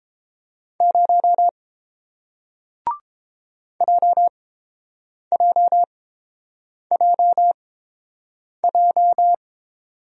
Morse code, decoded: 0 JJJJ

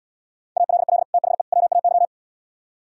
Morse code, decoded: ALL THE WAY

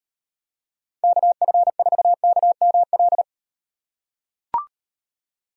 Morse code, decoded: KF4KML

